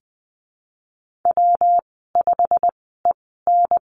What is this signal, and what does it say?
Morse code, heard as a keyed tone, about 20 wpm, spaced as for 15 wpm, 700 Hz. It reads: W5EN
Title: 193 Lesson 31g - Letter v - Call Signs 20wpm